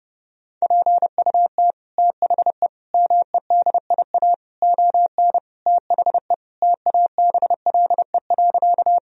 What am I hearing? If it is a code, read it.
Morse code, heard as a keyed tone, about 28 wpm, 700 Hz. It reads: PUT THE MEDIA ON THE TABLE.
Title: Sentences from Top 1000 Words - Review Part 4 - 30wpm